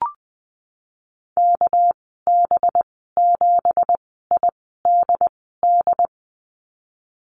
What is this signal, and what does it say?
Morse code, heard as a keyed tone, about 20 wpm, spaced as for 15 wpm, 700 Hz. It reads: KB7IDD
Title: 213 Lesson 34g - Number 8 - Call Signs 20wpm